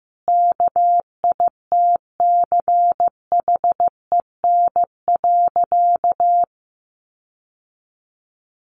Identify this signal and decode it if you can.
Morse code, heard as a keyed tone, about 15 wpm, 700 Hz. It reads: KITCHEN.
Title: Sentences from Top 1400 Words 15wpm